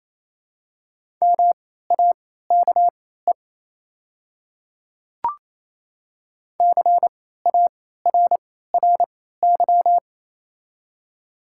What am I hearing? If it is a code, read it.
Morse code, heard as a keyed tone, about 28 wpm, spaced as for 16 wpm, 700 Hz. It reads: MAKE CARRY
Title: 3x Character Spacing - Top 500 Words - 28wpm